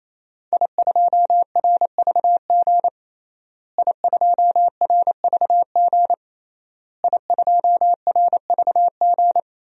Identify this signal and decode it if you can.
Morse code, heard as a keyed tone, about 28 wpm, 700 Hz. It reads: I2RVG I2RVG I2RVG
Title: DXCC-100 Call Signs - 5 Characters - Repeated 3x - 28wpm